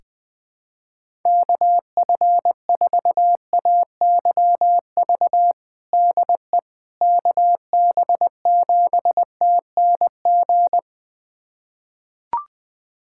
Morse code, heard as a keyed tone, about 20 wpm, 700 Hz. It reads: KF4AYV DE KB7TNG
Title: USA Callsign DE Callsign - 6 Characters - 20wpm